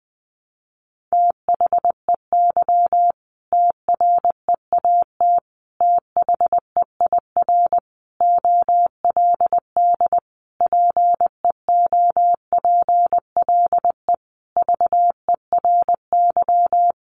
Morse code, heard as a keyed tone, about 20 wpm, 700 Hz. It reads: THEY TREAT THEIR OLD PEOPLE VERY
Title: Sentences from Top 1300 Words 20wpm